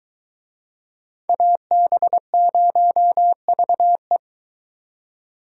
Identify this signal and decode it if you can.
Morse code, heard as a keyed tone, about 23 wpm, 700 Hz. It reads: AB0VE